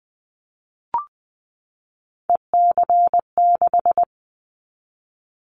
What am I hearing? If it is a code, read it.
Morse code, heard as a keyed tone, about 20 wpm, 700 Hz. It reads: EC6
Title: Callsign Prefixes for Top 50 DXCC - 20wpm